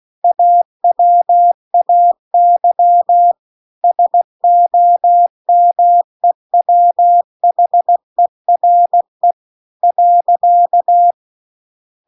Morse code, heard as a keyed tone, about 16 wpm, 700 Hz. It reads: AWAY SOMEWHERE .